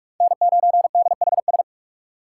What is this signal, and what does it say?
Morse code, heard as a keyed tone, about 42 wpm, 700 Hz. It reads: N9DHS